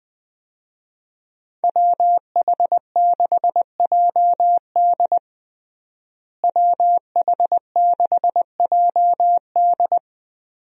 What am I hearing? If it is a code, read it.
Morse code, heard as a keyed tone, about 21 wpm, 700 Hz. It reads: WH6JD WH6JD
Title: USA Call Signs - 5 Characters - Repeated 3x - 20wpm